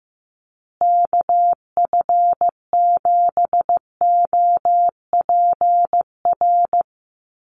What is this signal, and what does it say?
Morse code, heard as a keyed tone, about 15 wpm, 700 Hz. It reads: KF7OPR
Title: USA Call Signs Repeated 2x 15wpm